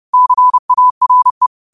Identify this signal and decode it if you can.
Morse code, heard as a keyed tone, about 30 wpm, 1000 Hz. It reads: CARE